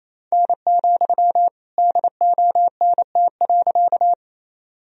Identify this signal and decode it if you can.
Morse code, heard as a keyed tone, about 28 wpm, 700 Hz. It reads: N, DONT.
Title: Sentences from Top 300 Words and Abbreviations 28wpm